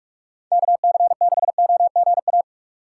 Morse code, heard as a keyed tone, about 44 wpm, 700 Hz. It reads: KC6OGA